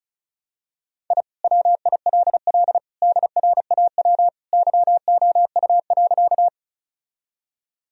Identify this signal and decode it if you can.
Morse code, heard as a keyed tone, about 36 wpm, 700 Hz. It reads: I WILL DRAW YOU.
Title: Sentences from Top 1000 Words - Review Part 1 - 35wpm